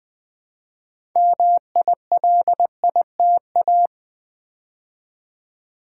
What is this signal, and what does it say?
Morse code, heard as a keyed tone, about 20 wpm, 700 Hz. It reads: MILITA
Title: Build A Word - 2 Letters At A Time - 8-Letter Words - 20wpm